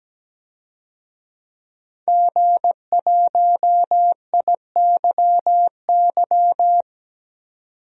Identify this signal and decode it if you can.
Morse code, heard as a keyed tone, about 17 wpm, 700 Hz. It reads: G1IYY
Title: DXCC-100 Call Signs - 5 Characters - 17wpm